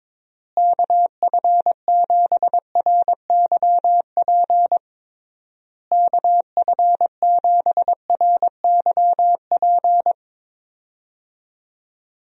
Morse code, heard as a keyed tone, about 22 wpm, 700 Hz. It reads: KF7RYP KF7RYP
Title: USA Call Signs - 6 Characters - Repeated 3x - 22wpm